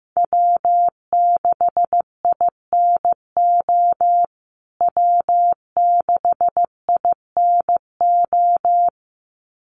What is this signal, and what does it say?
Morse code, heard as a keyed tone, about 15 wpm, 700 Hz. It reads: W6INO W6INO
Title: USA Call Signs Repeated 2x 15wpm